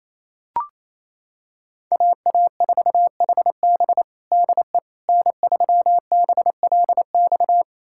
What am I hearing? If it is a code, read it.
Morse code, heard as a keyed tone, about 28 wpm, 700 Hz. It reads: AA4HB DE N3BLX
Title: USA Callsign DE Callsign - 5 Characters - 28wpm